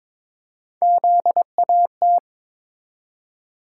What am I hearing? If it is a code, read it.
Morse code, heard as a keyed tone, about 21 wpm, 700 Hz. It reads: ZAT
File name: practice / 22wpm / Three Letter-Number Combinations - 22wpm